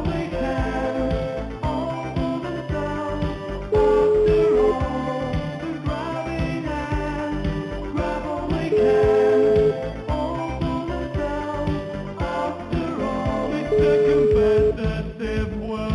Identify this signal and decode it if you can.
Ringback tone (Europe, 425 Hz), with noise.